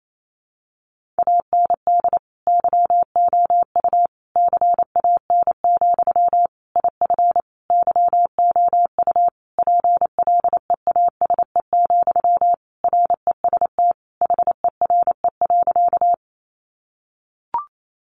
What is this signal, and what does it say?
Morse code, heard as a keyed tone, about 28 wpm, 700 Hz. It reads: AND YOU CAN, IF YOU PLEASE, REST HERE.
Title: Sentences from Top 1500 Words 28wpm